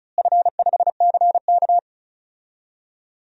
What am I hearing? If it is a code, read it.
Morse code, heard as a keyed tone, about 34 wpm, 700 Hz. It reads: F5CK
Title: DXCC-100 Call Signs - 4 Characters - Repeated 3x - 35wpm